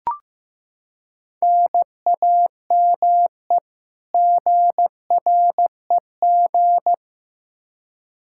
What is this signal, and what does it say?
Morse code, heard as a keyed tone, about 15 wpm, 700 Hz. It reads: NAME GREG
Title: Practice QSO - Exchange 1 - Beginning - 15wpm